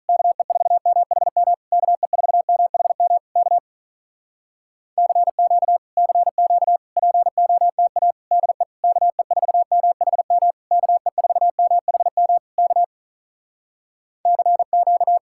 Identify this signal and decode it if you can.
Morse code, heard as a keyed tone, about 46 wpm, 700 Hz. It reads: KE4MHM KE4MHM K CQ CQ POTA DE KE4MHM KE4MHM K CQ